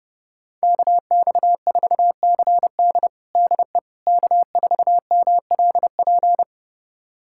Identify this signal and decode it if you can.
Morse code, heard as a keyed tone, about 32 wpm, 700 Hz. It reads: KX4CD DE K4MLP
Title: USA Callsign DE Callsign - 5 Characters - 30wpm